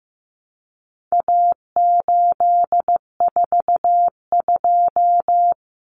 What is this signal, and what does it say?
Morse code, heard as a keyed tone, about 15 wpm, 700 Hz. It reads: A842
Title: Common Cut Numbers - 4-Digit 15wpm